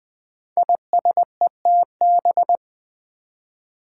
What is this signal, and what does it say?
Morse code, heard as a keyed tone, about 20 wpm, 700 Hz. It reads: ISETB
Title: Five Challenging Letter-Number Combinations 20wpm